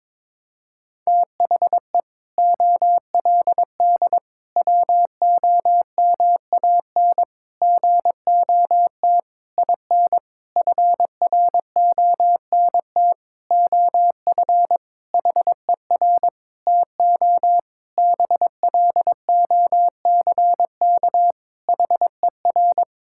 Morse code, heard as a keyed tone, about 22 wpm, 700 Hz. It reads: THE OLD WOMAN GOT IN FRONT OF HER TO BLOCK HER
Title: Sentences from Top 1600 Words 22wpm